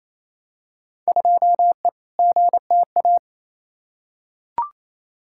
Morse code, heard as a keyed tone, about 28 wpm, 700 Hz. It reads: 2E GTA
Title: Field Day Exchanges Encore - 28wpm